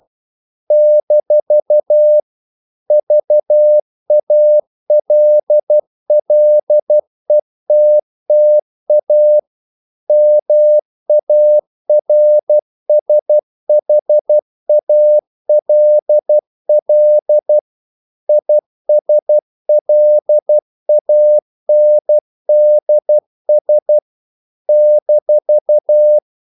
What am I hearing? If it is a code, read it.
Morse code, heard as a keyed tone, about 12 wpm, 600 Hz. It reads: - VALLETTA MARSHALL ISLANDS -
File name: CW / Practice / NationalCapitals12